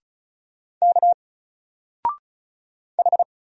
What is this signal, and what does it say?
Morse code, heard as a keyed tone, about 34 wpm, 700 Hz. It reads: K H
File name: Single-Letter-Number-35wpm